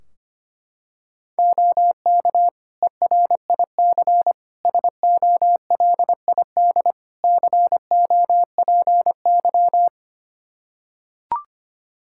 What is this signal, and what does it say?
Morse code, heard as a keyed tone, about 25 wpm, 700 Hz. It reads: OK ERIC SOLID COPY
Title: Practice QSO - Exchange 3 - More Info - 25wpm